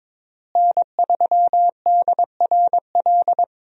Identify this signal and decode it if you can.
Morse code, heard as a keyed tone, about 22 wpm, 700 Hz. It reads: N3DRL
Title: USA Call Signs - 5 Characters - Repeated 2x - 22wpm